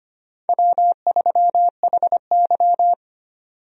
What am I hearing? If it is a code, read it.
Morse code, heard as a keyed tone, about 25 wpm, 700 Hz. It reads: W3HY
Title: USA Call Signs - 4 Characters - 25wpm